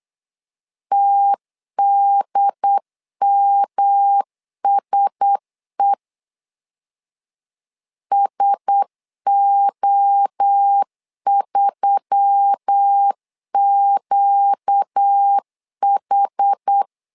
Morse code, heard as a keyed tone, about 8 wpm, 800 Hz. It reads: TDMSE SO3QH
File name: LEZIONI / 4 Q1S3 8wpm 3037